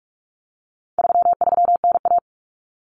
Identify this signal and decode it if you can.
Morse code, heard as a keyed tone, about 44 wpm, 700 Hz. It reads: 33NA
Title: Common Cut Numbers - 4-Digit 45wpm